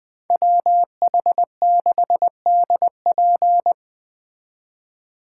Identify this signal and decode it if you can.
Morse code, heard as a keyed tone, about 20 wpm, 700 Hz. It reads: WH6DP
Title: Speed Racing - USA 2x2 Call Signs 17wpm